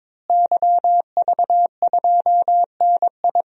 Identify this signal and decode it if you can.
Morse code, heard as a keyed tone, about 22 wpm, 700 Hz. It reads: YV2NI
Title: DXCC-100 Call Signs - 5 Characters - Repeated 2x - 22wpm